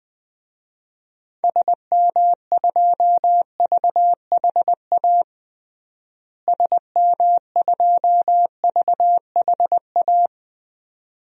Morse code, heard as a keyed tone, about 20 wpm, 700 Hz. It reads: SM2VHA SM2VHA
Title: DXCC-100 Call Signs - 6 Characters - Repeated 2x - 20wpm